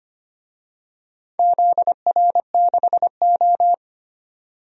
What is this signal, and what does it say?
Morse code, heard as a keyed tone, about 25 wpm, 700 Hz. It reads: ZR6O
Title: DXCC-100 Call Signs - 4 Characters - 25wpm